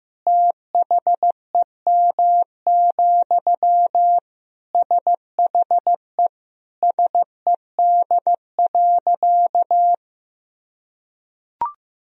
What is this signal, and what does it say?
Morse code, heard as a keyed tone, about 15 wpm, 700 Hz. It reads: THEM, SHE SED.
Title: Sentences from Top 400 Words and Abbreviations 15wpm